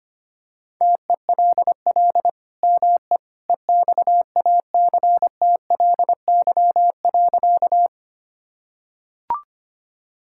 Morse code, heard as a keyed tone, about 25 wpm, 700 Hz. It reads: TELL ME EXACTLY.
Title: Sentences from Top 1000 Words - Review Part 6 - 25wpm